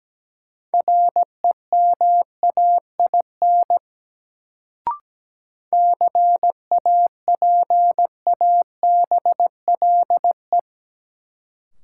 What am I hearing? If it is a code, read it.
Morse code, heard as a keyed tone, about 17 wpm, 700 Hz. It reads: REMAIN CAPABLE
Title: Sentences from Top 1200 Words - New Word Drill - 17wpm